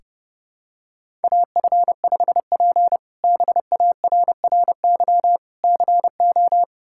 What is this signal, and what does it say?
Morse code, heard as a keyed tone, about 30 wpm, 700 Hz. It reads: AF5P BARRY CO
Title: K1USN SST QSO Practice - 30wpm